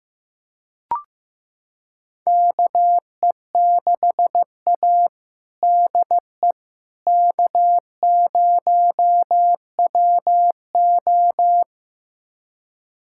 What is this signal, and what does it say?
Morse code, heard as a keyed tone, about 15 wpm, 700 Hz. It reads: KE6A DE K0WO
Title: USA Callsign DE Callsign - 4 Characters - 15wpm